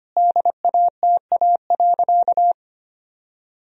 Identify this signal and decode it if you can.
Morse code, heard as a keyed tone, about 25 wpm, 700 Hz. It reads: DATA.
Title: Sentences from Top 1000 Words - Review Part 3 - 25wpm